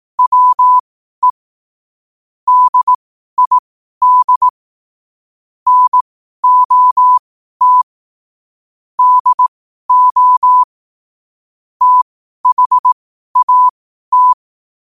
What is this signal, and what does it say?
Morse code, heard as a keyed tone, about 18 wpm, spaced as for 12 wpm, 1000 Hz. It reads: WE DID NOT DO THAT
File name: qotdfiles / 12wpm